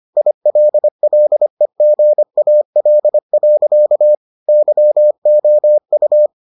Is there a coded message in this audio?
Yes: Morse code from 0.2 to 6.3 s, heard as a keyed tone, about 25 wpm, 600 Hz. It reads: ILLEGAL. YOU